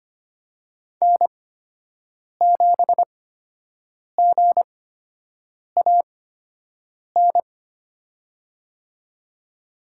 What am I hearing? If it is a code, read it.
Morse code, heard as a keyed tone, about 25 wpm, spaced as for 7 wpm, 700 Hz. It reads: N7GAN
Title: USA Call Signs - 5 Characters - 8x Character Spacing - 25wpm